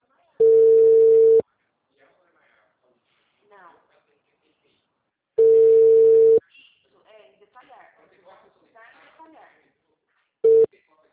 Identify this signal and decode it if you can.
Ringback tone (China, 450 Hz), with speech.